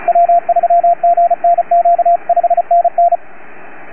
Morse code, heard as a keyed tone, about 33 wpm, 690 Hz, with radio noise. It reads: W3GNQ 5NN